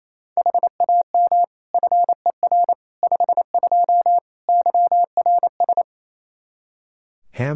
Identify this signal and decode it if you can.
Morse code, heard as a keyed tone, about 28 wpm, 700 Hz. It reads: HAM FER 52 YRS
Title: Practice QSO - Exchange 3 - More Info - 28wpm